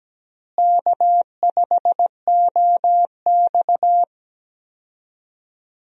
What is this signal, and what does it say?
Morse code, heard as a keyed tone, about 17 wpm, 700 Hz. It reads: K5OX
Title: USA Call Signs - 4 Characters - Repeated 3x - 17wpm